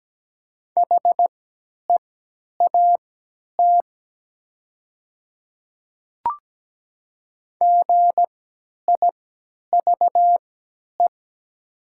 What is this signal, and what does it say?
Morse code, heard as a keyed tone, about 17 wpm, spaced as for 10 wpm, 700 Hz. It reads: HEAT GIVE
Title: 3x Character Spacing - Top 500 Words - 17wpm